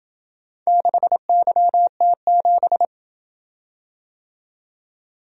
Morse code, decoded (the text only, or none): 6YT7